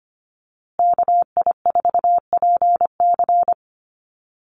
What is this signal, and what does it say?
Morse code, heard as a keyed tone, about 25 wpm, 700 Hz. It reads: KI4PC
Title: USA Call Signs Repeated 3x 25wpm